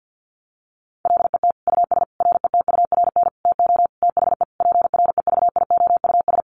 Morse code, heard as a keyed tone, about 50 wpm, 700 Hz. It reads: LET US RETURN TO THE PREVIOUS
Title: Sentences from Top 1500 Words 50wpm